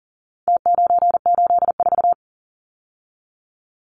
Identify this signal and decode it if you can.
Morse code, heard as a keyed tone, about 40 wpm, 700 Hz. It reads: T984
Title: Numbers - Contest Style - 4-Digit 40wpm